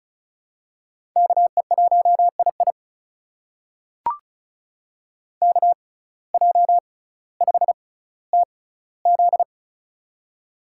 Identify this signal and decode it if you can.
Morse code, heard as a keyed tone, about 36 wpm, spaced as for 18 wpm, 700 Hz. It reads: KE1II KJ5TZ
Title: USA Call Signs - 5 Characters - 6x Character Spacing - 35wpm